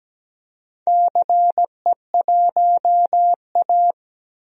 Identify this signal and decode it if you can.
Morse code, heard as a keyed tone, about 17 wpm, 700 Hz. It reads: CE1A
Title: DXCC-100 Call Signs - 4 Characters - Repeated 2x - 17wpm